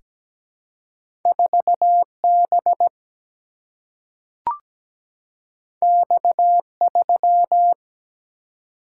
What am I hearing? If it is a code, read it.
Morse code, heard as a keyed tone, about 17 wpm, 700 Hz. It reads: 4 B X 3
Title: Single-Letter Single-Number 17wpm